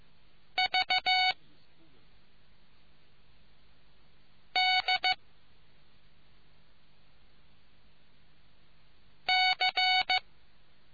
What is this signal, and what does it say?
Morse code, heard as a keyed tone, about 15 wpm, 2250 Hz. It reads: V D C